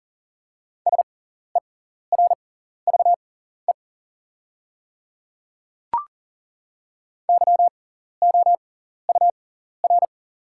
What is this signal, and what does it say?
Morse code, heard as a keyed tone, about 40 wpm, spaced as for 14 wpm, 700 Hz. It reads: SERVE YOUR
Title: 6x Character Spacing - Top 500 Words - 40wpm